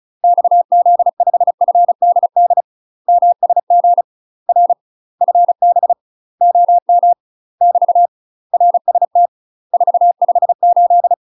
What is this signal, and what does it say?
Morse code, heard as a keyed tone, about 36 wpm, 700 Hz. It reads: XZ5FDD MSG R FB OM <BT> RST 458
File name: UK MASTER QSOs TRAINING / 35WPM / UK MASTER QSOs 0044